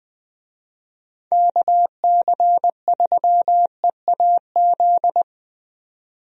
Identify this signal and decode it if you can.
Morse code, heard as a keyed tone, about 20 wpm, 700 Hz. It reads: KC3EAZ
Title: USA Call Signs - 6 Characters - 20wpm